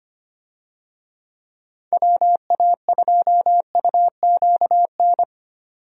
Morse code, heard as a keyed tone, about 25 wpm, 700 Hz. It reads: WA2UQN
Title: USA Call Signs - 6 Characters - 25wpm